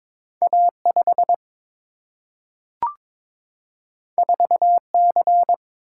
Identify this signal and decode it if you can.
Morse code, heard as a keyed tone, about 22 wpm, 700 Hz. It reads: A 5 4 C
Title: Single-Letter Single-Number 22wpm